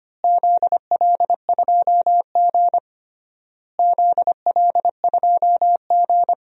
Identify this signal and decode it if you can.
Morse code, heard as a keyed tone, about 25 wpm, 700 Hz. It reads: ZL2G ZL2G
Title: DXCC-100 Call Signs - 4 Characters - Repeated 3x - 25wpm